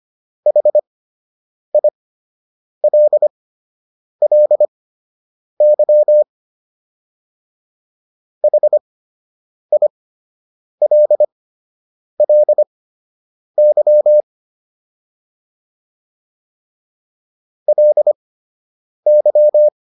Morse code, heard as a keyed tone, about 25 wpm, spaced as for 8 wpm, 600 Hz. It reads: HILLY HILLY LY